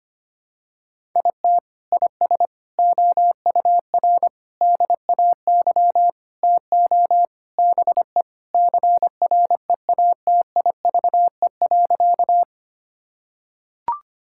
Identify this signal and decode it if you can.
Morse code, heard as a keyed tone, about 25 wpm, 700 Hz. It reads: IT IS OUR DAY TO BE CREATIVE.
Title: Sentences from Top 1000 Words - Review Part 5 - 25wpm